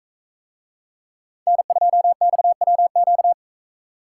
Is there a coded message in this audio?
Yes: Morse code from 1.5 to 3.3 s, heard as a keyed tone, about 41 wpm, 700 Hz. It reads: N2XWQ